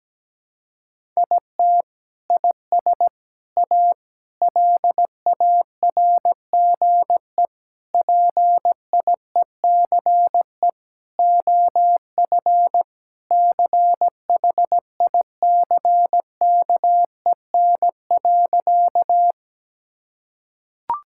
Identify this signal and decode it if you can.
Morse code, heard as a keyed tone, about 17 wpm, 700 Hz. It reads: IT IS A LARGE PIECE OF CHICKEN.
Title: Sentences from Top 1000 Words - Review Part 5 - 17wpm